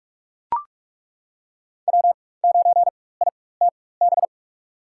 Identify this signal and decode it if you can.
Morse code, heard as a keyed tone, about 43 wpm, spaced as for 21 wpm, 700 Hz. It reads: W9ITB
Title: USA Call Signs - 4x Character Spacing - 45wpm